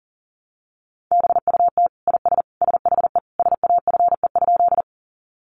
Morse code, heard as a keyed tone, about 40 wpm, 700 Hz. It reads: BUT IS SHE SAFE?
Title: Sentences from Top 900 Words 40wpm